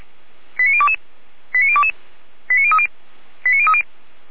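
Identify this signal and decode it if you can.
Five-tone selective call (ZVEI), heard as quick runs of five tones, tones of about 70 ms. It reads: 89022 89022 89030 89030
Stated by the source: Five-tone ZVEI selective call received on an SDR.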